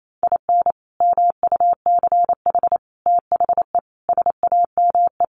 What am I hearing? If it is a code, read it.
Morse code, heard as a keyed tone, about 28 wpm, 700 Hz. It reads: IN MUCH THE SAME
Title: Sets of 5 Words Encore 28wpm